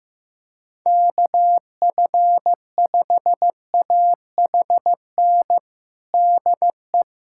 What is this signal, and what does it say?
Morse code, heard as a keyed tone, about 16 wpm, 700 Hz. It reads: KF5AHN DE
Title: USA Callsign DE Callsign 15wpm